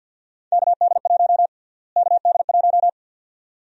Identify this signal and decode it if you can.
Morse code, heard as a keyed tone, about 51 wpm, 700 Hz. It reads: KD1 KD1
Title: USA Call Sign Prefixes - Repeated 2x - 50wpm